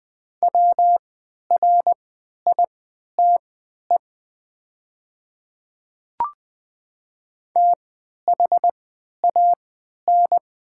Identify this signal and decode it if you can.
Morse code, heard as a keyed tone, about 20 wpm, spaced as for 11 wpm, 700 Hz. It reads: WRITE THAN
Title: 3x Character Spacing - Top 100 Words - 20wpm